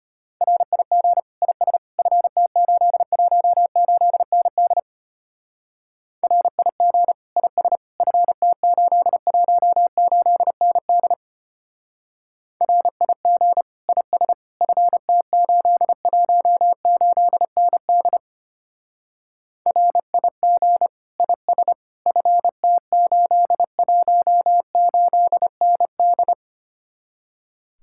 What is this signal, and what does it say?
Morse code, heard as a keyed tone, about 32 wpm, 700 Hz. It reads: RIG IS FT818ND RIG IS FT818ND RIG IS FT818ND RIG IS FT818ND